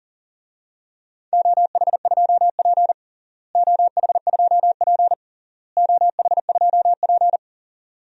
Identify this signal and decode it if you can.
Morse code, heard as a keyed tone, about 40 wpm, 700 Hz. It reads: OH2P OH2P OH2P
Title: DXCC-100 Call Signs - 4 Characters - Repeated 3x - 40wpm